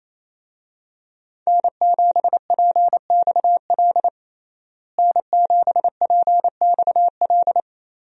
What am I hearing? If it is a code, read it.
Morse code, heard as a keyed tone, about 28 wpm, 700 Hz. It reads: N7PXL N7PXL
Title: USA Call Signs - 5 Characters - Repeated 3x - 28wpm